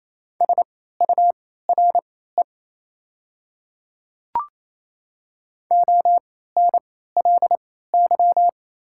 Morse code, heard as a keyed tone, about 28 wpm, spaced as for 16 wpm, 700 Hz. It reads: SURE ONLY
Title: 3x Character Spacing - Top 300 Words - 28wpm